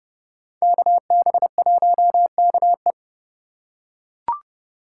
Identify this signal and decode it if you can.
Morse code, heard as a keyed tone, about 30 wpm, 700 Hz. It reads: KB1KE
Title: USA Call Signs - 5 Characters - 4x Character Spacing - 30wpm